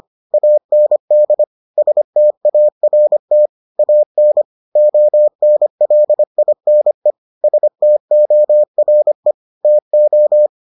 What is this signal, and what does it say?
Morse code, heard as a keyed tone, about 25 wpm, 600 Hz. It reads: AND START AN ONLINE STORE TO